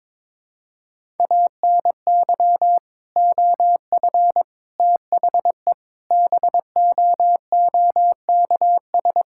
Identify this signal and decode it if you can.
Morse code, heard as a keyed tone, about 22 wpm, 700 Hz. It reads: ANY OF THE BOOKS
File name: practice / 22wpm / Sentences with 3-Word Phrases - 22wpm